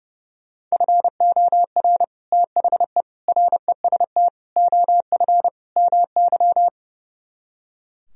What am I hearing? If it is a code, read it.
Morse code, heard as a keyed tone, about 36 wpm, 700 Hz. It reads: FOR THE REST OF MY
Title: Speed-Racing - Sets of 5 Words 30wpm